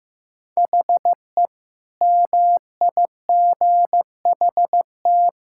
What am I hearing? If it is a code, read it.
Morse code, heard as a keyed tone, about 15 wpm, 700 Hz. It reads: HE MIGHT
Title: Sentences from Top 1600 Words 15wpm